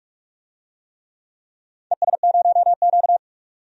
Morse code, heard as a keyed tone, about 46 wpm, 700 Hz. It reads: ES0Q
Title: DXCC-100 Call Signs - 4 Characters - 45wpm